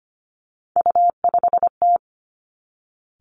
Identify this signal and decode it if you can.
Morse code, heard as a keyed tone, about 25 wpm, 700 Hz. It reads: U5T
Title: Three Challenging Letter-Number Combinations 25wpm